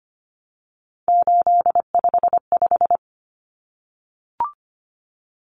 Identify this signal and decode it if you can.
Morse code, heard as a keyed tone, about 25 wpm, 700 Hz. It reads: 855
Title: Common Cut Numbers - 3-Digit 25wpm